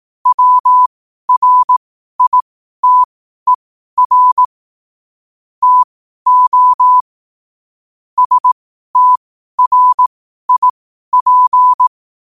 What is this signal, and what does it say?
Morse code, heard as a keyed tone, about 18 wpm, spaced as for 12 wpm, 1000 Hz. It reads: WRITER TO STRIP
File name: qotdfiles / 12wpm